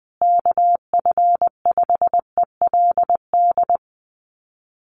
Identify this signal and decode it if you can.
Morse code, heard as a keyed tone, about 20 wpm, 700 Hz. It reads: KF5ELD
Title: USA Call Signs Repeated 2x 20wpm